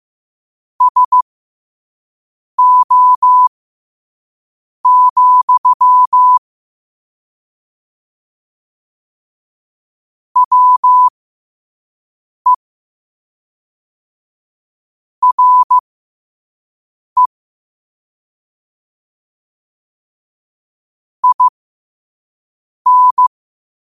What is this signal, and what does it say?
Morse code, heard as a keyed tone, about 15 wpm, spaced as for 5 wpm, 1000 Hz. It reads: SO, WE RE IN
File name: qotdfiles / 5wpm